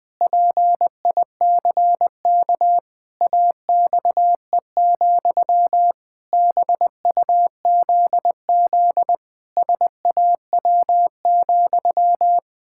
Morse code, heard as a keyed tone, about 20 wpm, 700 Hz. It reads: PICK AXE, BUZZ SAW,
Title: Pangram Challenge v2 - 20wpm